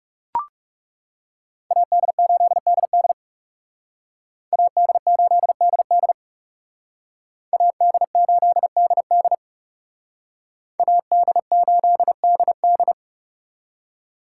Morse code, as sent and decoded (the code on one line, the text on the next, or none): .- -.. ---.. -.. -.. / .- -.. ---.. -.. -.. / .- -.. ---.. -.. -.. / .- -.. ---.. -.. -..
AD8DD AD8DD AD8DD AD8DD